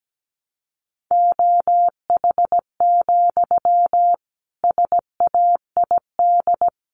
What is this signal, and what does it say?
Morse code, heard as a keyed tone, about 17 wpm, 700 Hz. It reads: OH, SAID
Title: Sentences from Top 500 Words 17wpm